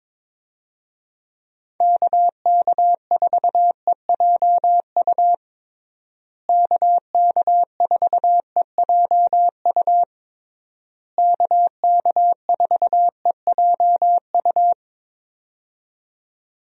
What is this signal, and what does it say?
Morse code, heard as a keyed tone, about 22 wpm, 700 Hz. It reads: KK4EJU KK4EJU KK4EJU
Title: USA Call Signs - 6 Characters - Repeated 3x - 22wpm